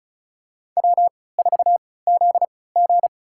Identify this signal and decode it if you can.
Morse code, heard as a keyed tone, about 34 wpm, spaced as for 20 wpm, 700 Hz. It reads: W4ZG
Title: USA Call Signs - 4 Characters - 3x Character Spacing - 35wpm